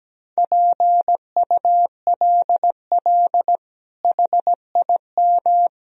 Morse code, heard as a keyed tone, about 17 wpm, 700 Hz. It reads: PULL HIM
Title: Sentences from Top 1000 Words - Review Part 2 - 17wpm